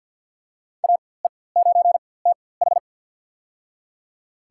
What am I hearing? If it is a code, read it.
Morse code, heard as a keyed tone, about 50 wpm, spaced as for 23 wpm, 700 Hz. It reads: AE9TH
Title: USA Call Signs - 4x Character Spacing - 50wpm